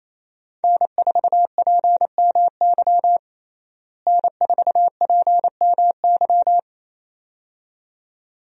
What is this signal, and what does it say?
Morse code, heard as a keyed tone, about 28 wpm, 700 Hz. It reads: N4PMY N4PMY
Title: USA Call Signs - 5 Characters - Repeated 2x - 28wpm